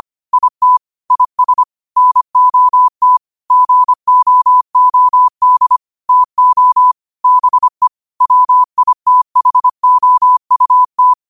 Morse code, heard as a keyed tone, about 25 wpm, 1000 Hz. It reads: IT IS NOT GOOD TO BE WITHOUT
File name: 25wpm